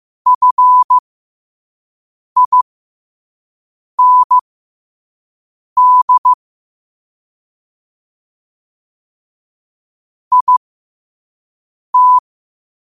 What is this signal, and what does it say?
Morse code, heard as a keyed tone, about 15 wpm, spaced as for 5 wpm, 1000 Hz. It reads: FIND IT
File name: qotdfiles / 5wpm